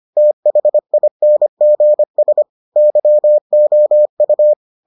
Morse code, heard as a keyed tone, about 25 wpm, 600 Hz. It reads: THINGS YOU